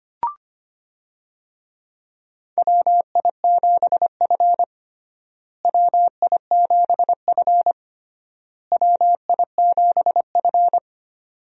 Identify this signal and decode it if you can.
Morse code, heard as a keyed tone, about 25 wpm, 700 Hz. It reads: WI7F WI7F WI7F
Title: USA Call Signs - 4 Characters - Repeated 3x - 25wpm